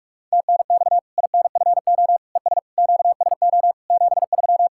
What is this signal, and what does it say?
Morse code, heard as a keyed tone, about 43 wpm, 700 Hz. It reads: TNX INFO ES QSO 73
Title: K1USN SST QSO Practice 45wpm